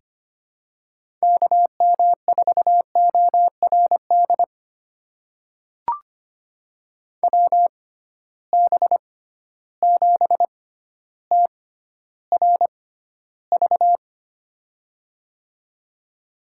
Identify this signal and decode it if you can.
Morse code, heard as a keyed tone, about 25 wpm, spaced as for 13 wpm, 700 Hz. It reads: KM4ORD WB7TRV
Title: USA Call Signs - 6 Characters - 6x Character Spacing - 25wpm